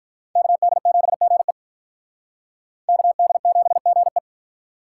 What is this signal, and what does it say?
Morse code, heard as a keyed tone, about 45 wpm, 700 Hz. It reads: KD7GE KD7GE